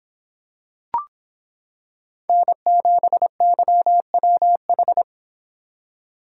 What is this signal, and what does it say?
Morse code, heard as a keyed tone, about 26 wpm, 700 Hz. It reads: N7YWH